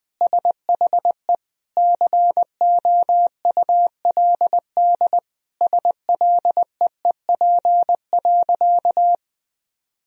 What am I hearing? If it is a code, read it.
Morse code, heard as a keyed tone, about 20 wpm, 700 Hz. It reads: SHE COULD SLEEP.
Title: Sentences from Top 500 Words and Abbreviations 20wpm